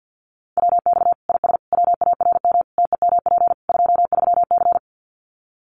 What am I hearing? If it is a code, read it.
Morse code, heard as a keyed tone, about 50 wpm, 700 Hz. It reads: WX IS WARM TEMP 23C
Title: Common QSO Phrases 50wpm